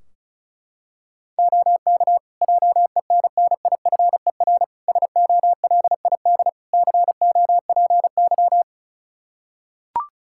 Morse code, heard as a keyed tone, about 36 wpm, 700 Hz. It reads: OK JENNIFER SOLID COPY
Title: Practice QSO - Exchange 3 - More Info - 35wpm